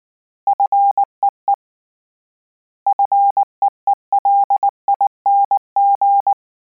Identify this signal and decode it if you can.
Morse code, heard as a keyed tone, about 19 wpm, 800 Hz. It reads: FEE FEELING
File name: Combinations / Combinations-mp3s / Combinations@18